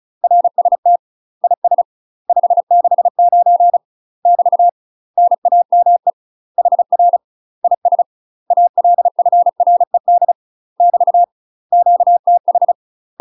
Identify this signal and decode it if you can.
Morse code, heard as a keyed tone, about 35 wpm, 700 Hz. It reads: RST IS 569 <BT> NAME HR IS ALFRED <BT> QTH